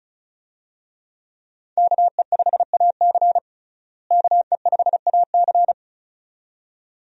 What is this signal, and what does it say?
Morse code, heard as a keyed tone, about 36 wpm, 700 Hz. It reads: KE5AC KE5AC
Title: USA Call Signs - 5 Characters - Repeated 2x - 35wpm